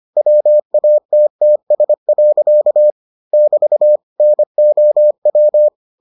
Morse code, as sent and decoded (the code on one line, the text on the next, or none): .-- .- - - ... .-.-.- / -...- / -. --- .--
WATTS. <BT> NOW